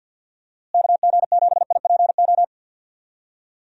Morse code, heard as a keyed tone, about 51 wpm, 700 Hz. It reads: KG7IPO
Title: USA Call Signs - 6 Characters - Repeated 2x - 50wpm